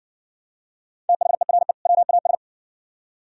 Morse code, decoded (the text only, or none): THERE WAS